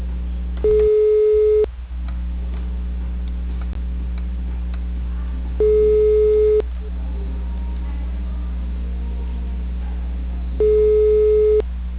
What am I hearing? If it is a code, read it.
Ringback tone (Europe, 425 Hz).